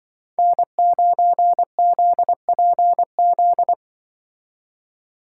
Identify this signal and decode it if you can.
Morse code, heard as a keyed tone, about 24 wpm, 700 Hz. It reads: N9ZPZ
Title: Speed Racing - USA 1x3 Call Signs 20wpm